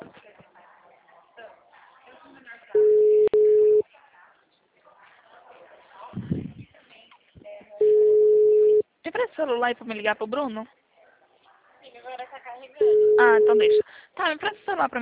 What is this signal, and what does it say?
Ringback tone (Europe, 425 Hz), with speech.